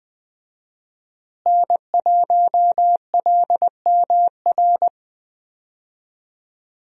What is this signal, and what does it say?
Morse code, heard as a keyed tone, about 20 wpm, 700 Hz. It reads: N1LMR
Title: USA Call Signs - 5 Characters - Repeated 2x - 20wpm